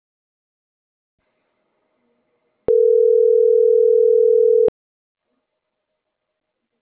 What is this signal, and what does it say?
Ringback tone (North America), 440 + 480 Hz.